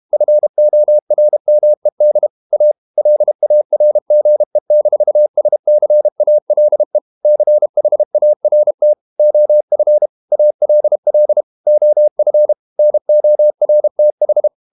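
Morse code, heard as a keyed tone, about 32 wpm, 600 Hz. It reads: FORMED A LARGE-SCALE CHART OF ALL OF NORTH
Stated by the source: Morse code practice at 32 wpm from CW med Gnister, reading Ken Follett's novel Never in English.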